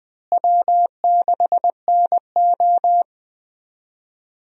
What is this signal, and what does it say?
Morse code, heard as a keyed tone, about 21 wpm, 700 Hz. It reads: W6NO
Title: USA Call Signs - 4 Characters - Repeated 3x - 20wpm